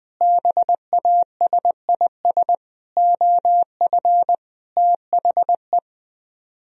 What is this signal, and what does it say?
Morse code, heard as a keyed tone, about 21 wpm, 700 Hz. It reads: BASIS OF THE
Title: Speed-Racing - Sets of 5 Words 20wpm